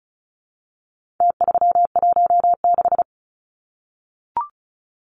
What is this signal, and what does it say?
Morse code, heard as a keyed tone, about 34 wpm, 700 Hz. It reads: T316
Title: Numbers - Contest Style - 4-Digit 35wpm